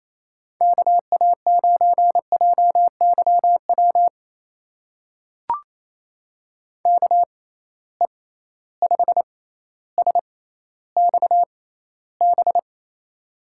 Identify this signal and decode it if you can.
Morse code, heard as a keyed tone, about 27 wpm, spaced as for 14 wpm, 700 Hz. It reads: KA9JYW KE5SXB
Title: USA Call Signs - 6 Characters - 6x Character Spacing - 28wpm